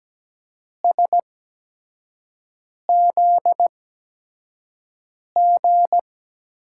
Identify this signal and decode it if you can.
Morse code, heard as a keyed tone, about 17 wpm, spaced as for 5 wpm, 700 Hz. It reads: SZG
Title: USA Call Signs - 6 Characters - 8x Character Spacing - 17wpm